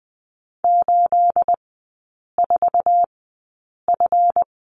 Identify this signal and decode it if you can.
Morse code, heard as a keyed tone, about 20 wpm, spaced as for 8 wpm, 690 Hz. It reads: 84F
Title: 212 Lesson 34e - Number 8 - Three Characters 20wpm